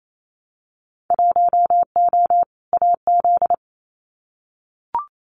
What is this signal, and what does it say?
Morse code, heard as a keyed tone, about 28 wpm, 700 Hz. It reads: 1O AZ
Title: Winter Field Day Exchanges 28wpm